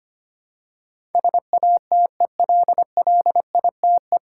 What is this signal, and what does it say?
Morse code, heard as a keyed tone, about 25 wpm, 700 Hz. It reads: SATELLITE
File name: Nine Letter Words - 25wpm